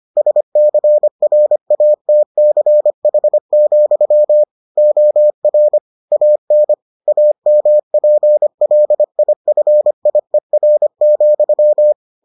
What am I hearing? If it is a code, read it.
Morse code, heard as a keyed tone, about 25 wpm, 600 Hz. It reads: SCRATCH, OR AN AMPLIFIER,